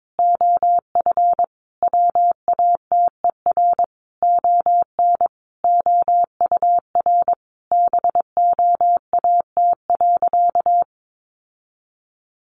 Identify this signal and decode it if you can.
Morse code, heard as a keyed tone, about 22 wpm, 700 Hz. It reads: OF WATER ON OUR BOAT.
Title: Sentences from Top 1400 Words 22wpm